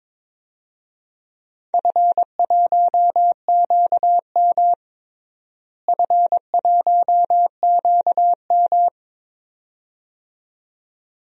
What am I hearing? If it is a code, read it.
Morse code, heard as a keyed tone, about 22 wpm, 700 Hz. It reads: F1QM F1QM
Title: DXCC-100 Call Signs - 4 Characters - Repeated 2x - 22wpm